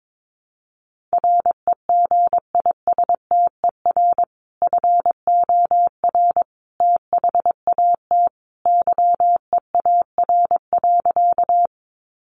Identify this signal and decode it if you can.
Morse code, heard as a keyed tone, about 22 wpm, 700 Hz. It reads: REGISTER FOR THAT YEAR.
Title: Sentences from Top 1200 Words 22wpm